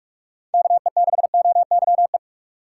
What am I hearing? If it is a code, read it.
Morse code, heard as a keyed tone, about 43 wpm, 700 Hz. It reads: KE6OYE
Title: USA Call Signs - 4x Character Spacing - 45wpm